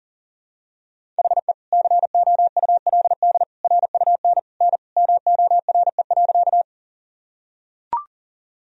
Morse code, heard as a keyed tone, about 40 wpm, 700 Hz. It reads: HE COULD RUN N MORE.
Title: Sentences from Top 300 Words and Abbreviations 40wpm